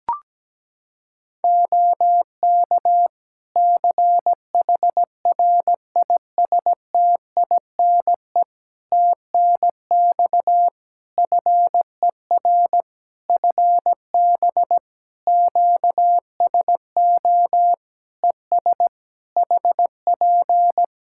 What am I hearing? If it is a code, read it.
Morse code, heard as a keyed tone, about 17 wpm, 700 Hz. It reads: OK CHRISTINE TNX FER FB QSO ES HP
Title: Practice QSO - Exchange 4 - Ending - 17wpm